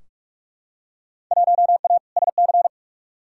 Morse code, heard as a keyed tone, about 46 wpm, 700 Hz. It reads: 1A SC
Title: Field Day Exchanges - 45wpm